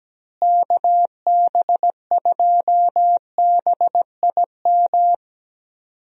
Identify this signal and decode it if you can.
Morse code, heard as a keyed tone, about 17 wpm, 700 Hz. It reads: KB2BIM
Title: USA Call Signs - 6 Characters - Repeated 2x - 17wpm